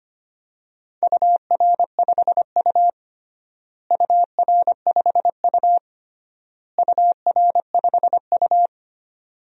Morse code, heard as a keyed tone, about 25 wpm, 700 Hz. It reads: UR5U UR5U UR5U
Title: DXCC-100 Call Signs - 4 Characters - Repeated 3x - 25wpm